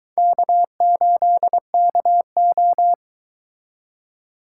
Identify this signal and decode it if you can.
Morse code, heard as a keyed tone, about 22 wpm, 700 Hz. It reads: K8KO